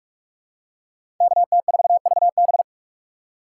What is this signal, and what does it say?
Morse code, heard as a keyed tone, about 46 wpm, 700 Hz. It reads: KT4VB
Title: USA Call Signs - 5 Characters - 2x Character Spacing - 45wpm